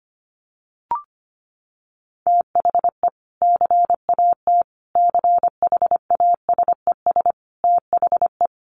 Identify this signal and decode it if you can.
Morse code, heard as a keyed tone, about 25 wpm, 700 Hz. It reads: THE CAT CHASES THE
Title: Additional Easy Phrases 25wpm